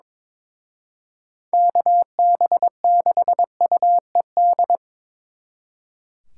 Morse code, heard as a keyed tone, about 27 wpm, 700 Hz. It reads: KB6UED
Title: Speed Racing - USA 2x3 Call Signs 22wpm